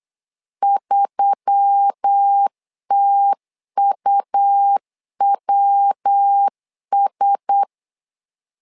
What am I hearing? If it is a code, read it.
Morse code, heard as a keyed tone, about 8 wpm, 800 Hz. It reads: 3TUWS
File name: LEZIONI / 4 Q1S3 8wpm 3037